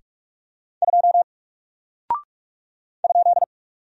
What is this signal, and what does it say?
Morse code, heard as a keyed tone, about 44 wpm, 700 Hz. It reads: 2 ?
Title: Single Character (Letter:Number:Punctuation:Prosign) Alt 45wpm